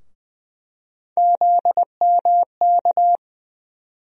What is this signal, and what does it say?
Morse code, heard as a keyed tone, about 20 wpm, 700 Hz. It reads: ZMK
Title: USA Call Sign Postfixes - 3 Letters 20wpm